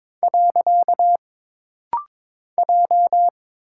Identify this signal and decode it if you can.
Morse code, heard as a keyed tone, about 22 wpm, 700 Hz. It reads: . J
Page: This rapid-fire timing is geared to push your ICR (Instant Character Recognition) skills.